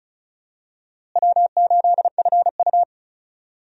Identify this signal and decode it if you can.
Morse code, heard as a keyed tone, about 36 wpm, 700 Hz. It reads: W8FU
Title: USA Call Signs - 4 Characters - Repeated 2x - 35wpm